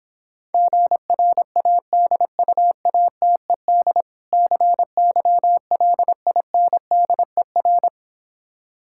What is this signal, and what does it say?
Morse code, heard as a keyed tone, about 26 wpm, 700 Hz. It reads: GRADUATED CYLINDER